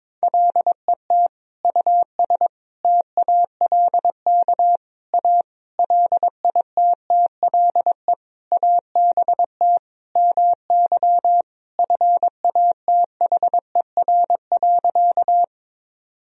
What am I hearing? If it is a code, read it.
Morse code, heard as a keyed tone, about 22 wpm, 700 Hz. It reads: LET US TALK A LITTLE ABT MY FATHER.
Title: Sentences from Top 400 Words and Abbreviations 22wpm